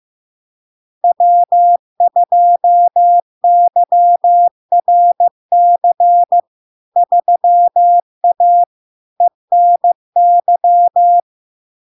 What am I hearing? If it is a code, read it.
Morse code, heard as a keyed tone, about 15 wpm, 700 Hz. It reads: W2YRC 3A ENY
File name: LARK-FD-2022-Log-in-CW-15wpm-700-Hz